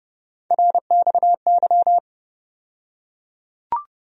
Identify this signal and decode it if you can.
Morse code, heard as a keyed tone, about 30 wpm, 700 Hz. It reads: RXY
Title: USA Call Sign Postfixes - 3 Letters 30wpm